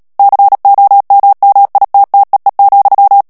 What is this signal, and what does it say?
Morse code, heard as a keyed tone, about 37 wpm, 800 Hz. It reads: COMMITTEE,
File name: morse mp3 / ANGLAIS texte clair 3 vitesse 35 mots minute